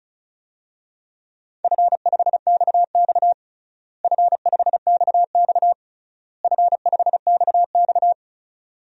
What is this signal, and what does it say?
Morse code, heard as a keyed tone, about 36 wpm, 700 Hz. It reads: F5XX F5XX F5XX
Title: DXCC-100 Call Signs - 4 Characters - Repeated 3x - 35wpm